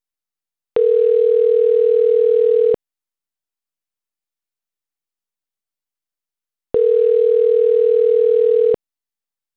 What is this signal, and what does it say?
Ringback tone (North America), 440 + 480 Hz.